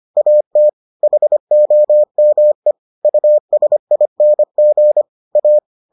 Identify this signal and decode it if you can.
Morse code, heard as a keyed tone, about 25 wpm, 600 Hz. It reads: AT HOME USING A